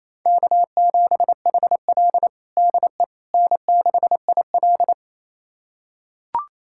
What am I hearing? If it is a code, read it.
Morse code, heard as a keyed tone, about 27 wpm, 690 Hz. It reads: K7HL DE N6IL
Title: USA Callsign DE Callsign - 4 Characters - 28wpm